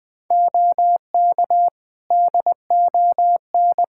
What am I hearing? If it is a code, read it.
Morse code, heard as a keyed tone, about 20 wpm, 700 Hz. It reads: OK DON
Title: Practice QSO - Exchange 3 - More Info - 20wpm